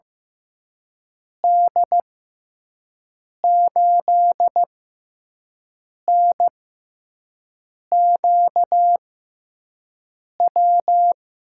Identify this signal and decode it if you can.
Morse code, heard as a keyed tone, about 15 wpm, spaced as for 5 wpm, 700 Hz. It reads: D8NQW
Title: USA Call Signs - 6 Characters - 6x Character Spacing - 15wpm